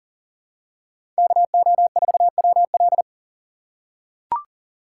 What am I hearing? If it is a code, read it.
Morse code, heard as a keyed tone, about 40 wpm, 700 Hz. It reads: KO4WL
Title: USA Call Signs - 5 Characters - 2x Character Spacing - 40wpm